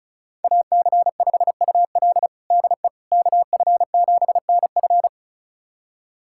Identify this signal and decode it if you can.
Morse code, heard as a keyed tone, about 36 wpm, 700 Hz. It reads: AC5UL DE KF7NF
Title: USA Callsign DE Callsign 35wpm